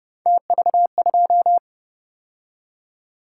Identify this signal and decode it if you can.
Morse code, heard as a keyed tone, about 32 wpm, 700 Hz. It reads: TV2